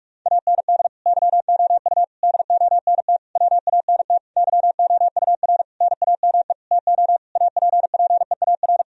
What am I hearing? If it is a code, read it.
Morse code, heard as a keyed tone, about 43 wpm, 700 Hz. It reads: AND YOU DONT WANT YOUR NAME TO APPEAR
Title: Sentences from Top 1000 Words - Review Part 2 - 45wpm